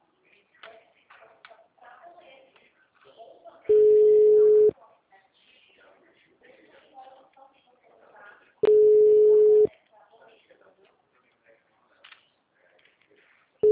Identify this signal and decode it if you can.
Ringback tone (Europe, 425 Hz), with speech.
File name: out- -2014- - - .18195